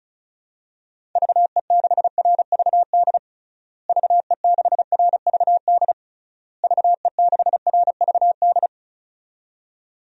Morse code, heard as a keyed tone, about 34 wpm, 700 Hz. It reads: VE6RVD VE6RVD VE6RVD
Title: DXCC-100 Call Signs - 6 Characters - Repeated 3x - 35wpm